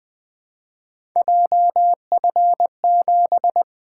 Morse code, heard as a keyed tone, about 20 wpm, 700 Hz. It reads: JF7
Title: Three Letter-Number Combinations - 20wpm